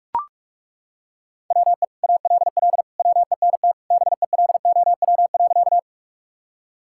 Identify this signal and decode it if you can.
Morse code, heard as a keyed tone, about 41 wpm, 700 Hz. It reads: WE ALL WENT BELOW.
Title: Sentences from Top 1000 Words - Review Part 5 - 45wpm